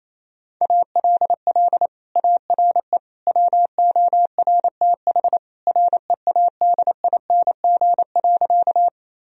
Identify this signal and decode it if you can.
Morse code, heard as a keyed tone, about 27 wpm, 700 Hz. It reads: ALL ARE WORTH READING.
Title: Sentences from Top 1000 Words - Review Part 3 - 28wpm